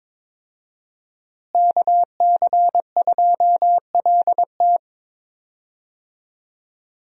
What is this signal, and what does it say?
Morse code, heard as a keyed tone, about 22 wpm, 700 Hz. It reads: KC2LT
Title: USA Call Signs - 5 Characters - Repeated 2x - 22wpm